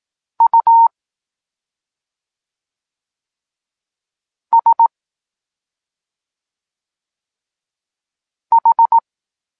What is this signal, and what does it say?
Morse code, heard as a keyed tone, about 18 wpm, 920 Hz. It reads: U S H
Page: Tekst 8 – doskonalenie odbioru poznanych liter